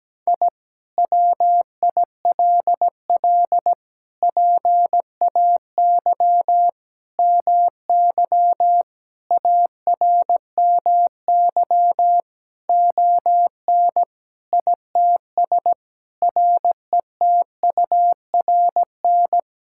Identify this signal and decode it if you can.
Morse code, heard as a keyed tone, about 17 wpm, 700 Hz. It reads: I WILL PAY MY ARMY ON ITS RETURN
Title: Sentences from Top 1000 Words - Review Part 5 - 17wpm